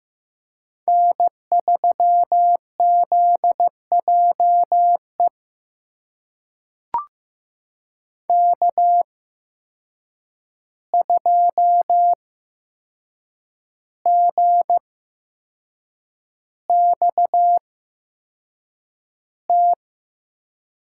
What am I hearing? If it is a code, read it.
Morse code, heard as a keyed tone, about 15 wpm, spaced as for 6 wpm, 700 Hz. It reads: N3ZJE K2GXT
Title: USA Call Signs - 8x Character Spacing - 15wpm